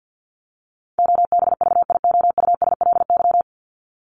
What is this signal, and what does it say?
Morse code, heard as a keyed tone, about 50 wpm, 700 Hz. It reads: OBVIOUSLY
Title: LY Suffix - Top 100 Words with Base Words 50wpm